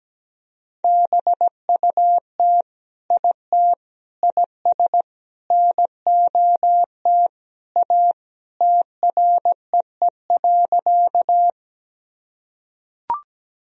Morse code, heard as a keyed tone, about 17 wpm, 700 Hz. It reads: BUT IT IS NOT A TREE.
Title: Sentences from Top 1000 Words - Review Part 1 - 17wpm